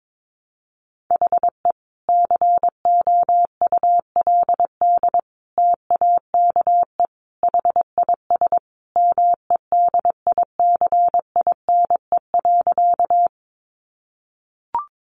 Morse code, heard as a keyed tone, about 22 wpm, 700 Hz. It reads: HE COULD TAKE HIS MEDICINE.
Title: Sentences from Top 900 Words 22wpm